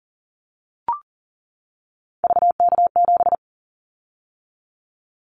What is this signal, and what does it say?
Morse code, heard as a keyed tone, about 38 wpm, 700 Hz. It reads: VK7
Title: Callsign Prefixes for Top 50 DXCC - 40wpm